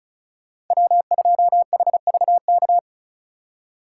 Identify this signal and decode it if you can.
Morse code, heard as a keyed tone, about 47 wpm, spaced as for 41 wpm, 700 Hz. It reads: W2HVK
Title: Speed Racing - USA 1x3 Call Signs 35wpm